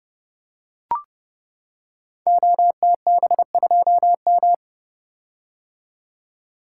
Morse code, heard as a keyed tone, about 30 wpm, 700 Hz. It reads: OTB2M
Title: Code Groups - 5 Letter:Numbers - 30wpm